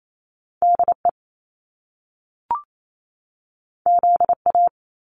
Morse code, heard as a keyed tone, about 28 wpm, 700 Hz. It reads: D E Z A
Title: Top Two Letter Combinations in Words 28wpm